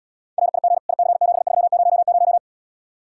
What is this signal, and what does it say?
Morse code, heard as a keyed tone, about 79 wpm, 700 Hz. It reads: RIG IC7300